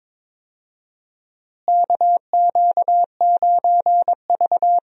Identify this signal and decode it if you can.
Morse code, heard as a keyed tone, about 22 wpm, 700 Hz. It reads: KQ9V
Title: USA Call Signs - 4 Characters - Repeated 3x - 22wpm